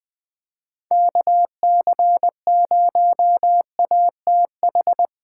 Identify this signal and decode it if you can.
Morse code, heard as a keyed tone, about 19 wpm, 700 Hz. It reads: KC0ATH
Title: Speed Racing - USA 2x3 Call Signs 17wpm